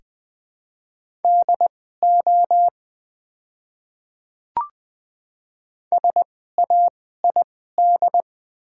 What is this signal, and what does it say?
Morse code, heard as a keyed tone, about 20 wpm, spaced as for 14 wpm, 700 Hz. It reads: DO SAID
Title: 2x Character Spacing - Top 100 Words - 20wpm